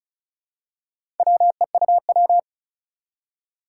Morse code, heard as a keyed tone, about 36 wpm, 700 Hz. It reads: WEUW
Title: Four Challenging Letter-Number Combinations 35wpm